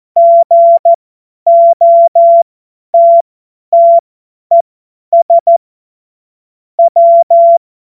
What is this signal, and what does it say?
Morse code, heard as a keyed tone, about 14 wpm, spaced as for 10 wpm, 680 Hz. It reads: GOTTES W
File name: Witz-19-August-2025-50-F70